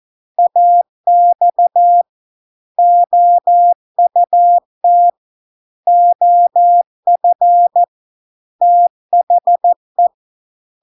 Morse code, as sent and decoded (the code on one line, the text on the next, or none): .- -..- / --- ..- - / --- ..-. / - .... .
AX OUT OF THE